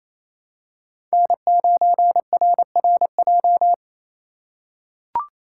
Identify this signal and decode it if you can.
Morse code, heard as a keyed tone, about 28 wpm, 700 Hz. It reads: N9RRJ
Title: USA Call Signs - 5 Characters - 2x Character Spacing - 28wpm